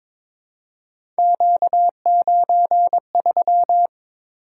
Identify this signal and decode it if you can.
Morse code, heard as a keyed tone, about 22 wpm, 700 Hz. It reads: Q93
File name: practice / 22wpm / Three Letter-Number Combinations - 22wpm